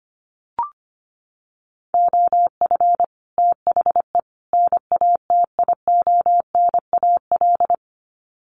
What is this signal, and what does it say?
Morse code, heard as a keyed tone, about 25 wpm, 700 Hz. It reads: OF THE NATIONAL
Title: Sets of 3 Words 25wpm